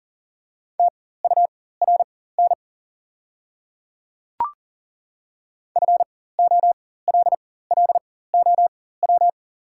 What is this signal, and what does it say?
Morse code, heard as a keyed tone, about 40 wpm, spaced as for 19 wpm, 700 Hz. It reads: TURN FOLLOW
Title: 4x Character Spacing - Top 200 Words - 40wpm